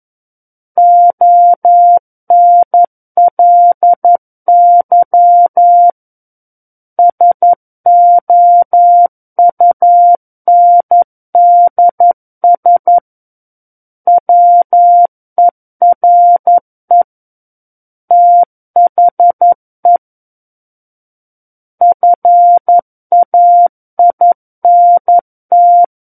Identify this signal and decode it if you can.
Morse code, heard as a keyed tone, about 11 wpm, 700 Hz. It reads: ONLY SOUNDS WERE THE FAINT
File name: Book One / 02-Chapter 2 - 11 WPM